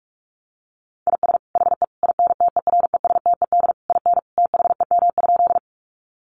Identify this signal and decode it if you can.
Morse code, heard as a keyed tone, about 44 wpm, 700 Hz. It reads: IS HE INTERESTED IN THEM?
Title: Sentences from Top 900 Words 45wpm